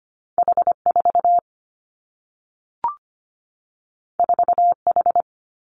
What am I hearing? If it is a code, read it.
Morse code, heard as a keyed tone, about 25 wpm, 700 Hz. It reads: H 4 4 H
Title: Two Challenging Letter-Number Combinations 25wpm